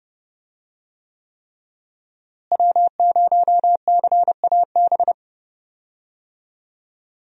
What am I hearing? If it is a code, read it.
Morse code, heard as a keyed tone, about 30 wpm, 700 Hz. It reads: W0CAB
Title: USA Call Signs - 5 Characters - Repeated 3x - 30wpm